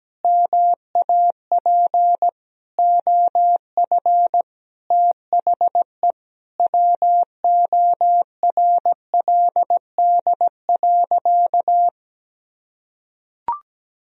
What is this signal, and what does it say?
Morse code, heard as a keyed tone, about 17 wpm, 700 Hz. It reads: MAP OF THE WORLD.
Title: Sentences from Top 1000 Words - Review Part 2 - 17wpm